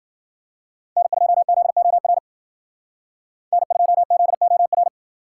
Morse code, heard as a keyed tone, about 48 wpm, 700 Hz. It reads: N2ZOR N2ZOR
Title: Speed Racing - USA Call Signs 40wpm